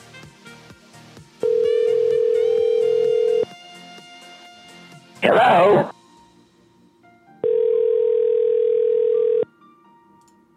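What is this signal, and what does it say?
Ringback tone (North America), 440 + 480 Hz.
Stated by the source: Prank phone calls made on a show streamed live on YouTube.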